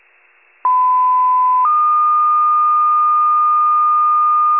Two-tone page, heard as a one-second tone, then a three-second tone, about 1010 then 1220 Hz. It is heard from 0.6 to 4.6 s.
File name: pager / 94 RESCUE OFC / wcalert